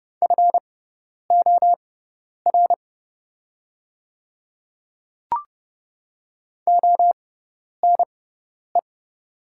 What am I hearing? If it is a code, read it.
Morse code, heard as a keyed tone, about 30 wpm, spaced as for 10 wpm, 700 Hz. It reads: FOR ONE
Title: 6x Character Spacing - Top 100 Words - 30wpm